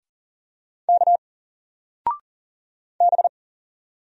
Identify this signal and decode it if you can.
Morse code, heard as a keyed tone, about 42 wpm, 700 Hz. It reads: K B
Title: Single Letter-Number 40wpm